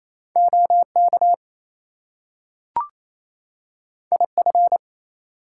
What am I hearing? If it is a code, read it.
Morse code, heard as a keyed tone, about 28 wpm, 700 Hz. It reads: OK IF
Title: Two Letter Words - 28wpm